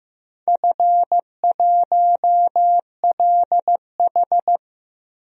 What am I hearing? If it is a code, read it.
Morse code, heard as a keyed tone, about 15 wpm, 700 Hz. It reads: F1LH
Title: DXCC-100 Call Signs - 4 Characters - 15wpm